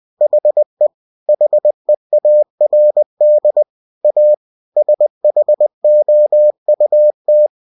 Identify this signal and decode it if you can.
Morse code, heard as a keyed tone, about 20 wpm, 600 Hz. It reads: HE HEARD A SHOUT